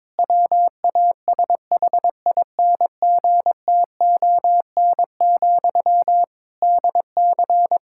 Morse code, heard as a keyed tone, about 21 wpm, 700 Hz. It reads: WASHINGTON, DC
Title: Practice QSO - Exchange 1 - Beginning - 22wpm